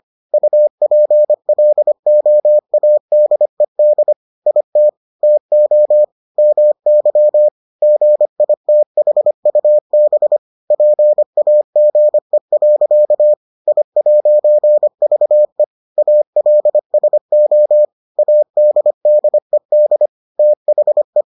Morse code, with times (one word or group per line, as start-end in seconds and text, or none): 0.3-4.1 UPLOADED
4.5-4.9 IT
5.2-6.0 TO
6.4-7.5 MY
7.8-10.4 GITHUB
10.7-13.3 PAGE.
13.7-15.6 I'VE
16.0-17.9 ALSO
18.2-20.1 ADDED
20.4-21.2 THE